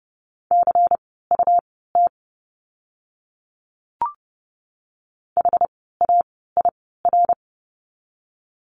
Morse code, heard as a keyed tone, about 30 wpm, spaced as for 17 wpm, 700 Hz. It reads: CUT HAIR
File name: Lesson 17f - Letter u - Words 30wpm